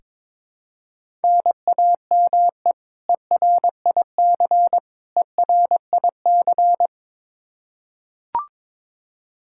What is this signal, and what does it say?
Morse code, heard as a keyed tone, about 21 wpm, 700 Hz. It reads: NAME ERIC ERIC
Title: Practice QSO - Exchange 1 - Beginning - 22wpm